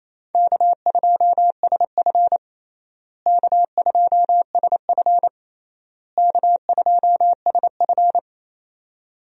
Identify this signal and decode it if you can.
Morse code, heard as a keyed tone, about 28 wpm, 700 Hz. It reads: K2SF K2SF K2SF
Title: USA Call Signs - 4 Characters - Repeated 3x - 28wpm